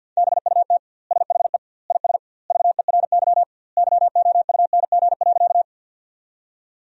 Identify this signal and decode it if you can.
Morse code, heard as a keyed tone, about 52 wpm, 700 Hz. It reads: BUT SHE IS VERY YOUNG.
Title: 2x Word-Spacing - Sentences from Top 400 Words - 50wpm